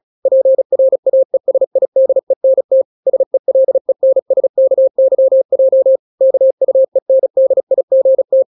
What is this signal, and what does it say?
Morse code, heard as a keyed tone, about 36 wpm, 530 Hz. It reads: PRAESIDENT SELENSKYJ KUENDIGT
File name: daily / 35wpm